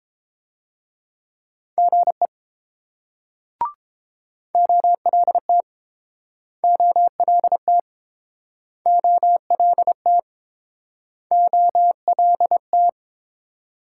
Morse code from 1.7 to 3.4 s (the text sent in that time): GE